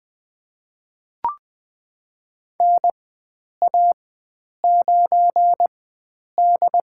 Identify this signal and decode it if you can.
Morse code, heard as a keyed tone, about 20 wpm, spaced as for 9 wpm, 700 Hz. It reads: NA9D
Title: USA Call Signs - 4 Characters - 4x Character Spacing - 20wpm